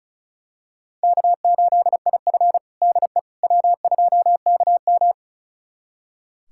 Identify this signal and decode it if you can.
Morse code, heard as a keyed tone, about 36 wpm, 700 Hz. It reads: K8IF DE W2KM
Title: USA Callsign DE Callsign - 4 Characters - 35wpm